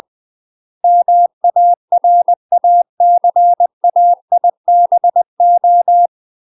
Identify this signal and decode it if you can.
Morse code, heard as a keyed tone, about 20 wpm, 700 Hz. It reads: MARACAIBO
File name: UK MASTER QSOs TRAINING / 20WPM / UK MASTER QSOs 0012